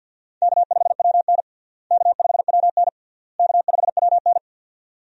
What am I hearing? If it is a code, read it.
Morse code, heard as a keyed tone, about 47 wpm, 700 Hz. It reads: K5WN K5WN K5WN
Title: USA Call Signs - 4 Characters - Repeated 3x - 50wpm